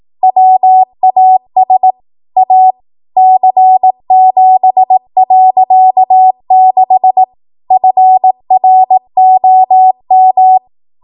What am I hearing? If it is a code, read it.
Morse code, heard as a keyed tone, about 18 wpm, 750 Hz. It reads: WAS A C7.6 FROM
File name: Morse / 18WPM